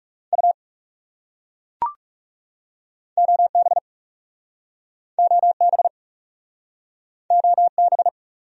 Morse code, heard as a keyed tone, about 38 wpm, 700 Hz. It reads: U OB OB OB